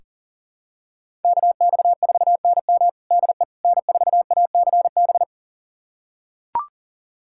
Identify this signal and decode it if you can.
Morse code, heard as a keyed tone, about 40 wpm, 700 Hz. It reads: KX4NM DE N4ACB
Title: USA Callsign DE Callsign 40wpm